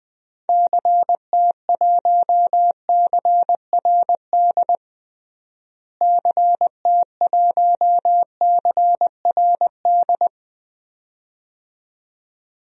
Morse code, heard as a keyed tone, about 20 wpm, 700 Hz. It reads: CT1CRD CT1CRD
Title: DXCC-100 Call Signs - 6 Characters - Repeated 2x - 20wpm